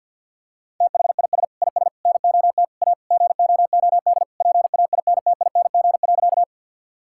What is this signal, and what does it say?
Morse code, heard as a keyed tone, about 47 wpm, 700 Hz. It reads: THIS IS NOT A GOOD PAINTING.
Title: Sentences from Top 1000 Words - Review Part 6 - 50wpm